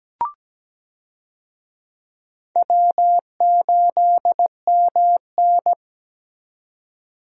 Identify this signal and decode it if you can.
Morse code, heard as a keyed tone, about 17 wpm, 700 Hz. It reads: W8MN
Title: USA Call Signs - 4 Characters - 17wpm